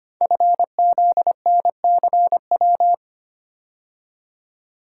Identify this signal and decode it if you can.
Morse code, heard as a keyed tone, about 25 wpm, 700 Hz. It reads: FZNCW
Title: Code Groups - 5 Letters - 25wpm